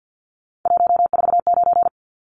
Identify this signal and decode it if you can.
Morse code, heard as a keyed tone, about 50 wpm, 700 Hz. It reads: 149
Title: Numbers - 3-Digit 50wpm